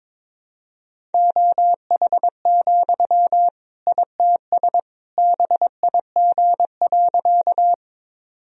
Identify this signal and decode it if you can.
Morse code, heard as a keyed tone, about 22 wpm, 700 Hz. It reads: OH, ITS BIG.
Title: Sentences from Top 1000 Words - Review Part 1 - 22wpm